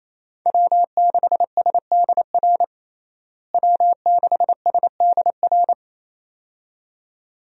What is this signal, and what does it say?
Morse code, heard as a keyed tone, about 28 wpm, 700 Hz. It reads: W6SDR W6SDR
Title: USA Call Signs - 5 Characters - Repeated 2x - 28wpm